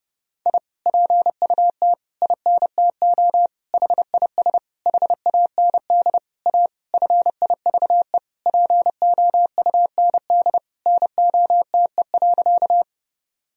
Morse code, heard as a keyed tone, about 30 wpm, 700 Hz. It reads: I PUT INTO HIS HAND A FIVE POUND NOTE.
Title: Sentences from Top 1000 Words - Review Part 2 - 30wpm